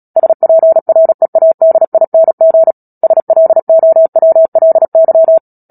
Morse code, heard as a keyed tone, about 37 wpm, 650 Hz. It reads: SPREADING SLOWLY
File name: Book Two / 10-Chapter 10 - Epilogue - 36 WPM